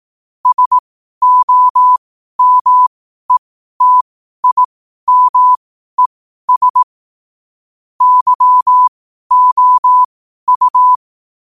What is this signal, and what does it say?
Morse code, heard as a keyed tone, about 18 wpm, spaced as for 12 wpm, 1000 Hz. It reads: SOMETIMES YOU
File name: qotdfiles / 12wpm